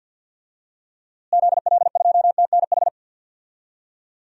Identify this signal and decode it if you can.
Morse code, heard as a keyed tone, about 47 wpm, 700 Hz. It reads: ZL2TNH
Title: DXCC-100 Call Signs - 6 Characters - Repeated 2x - 50wpm